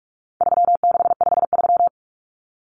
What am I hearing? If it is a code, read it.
Morse code, heard as a keyed tone, about 44 wpm, 700 Hz. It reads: 3653
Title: Common Cut Numbers - 4-Digit 45wpm